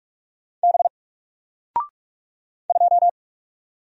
Morse code, heard as a keyed tone, about 44 wpm, 700 Hz. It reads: B 2
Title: Single Character (Letter:Number:Punctuation:Prosign) Alt 45wpm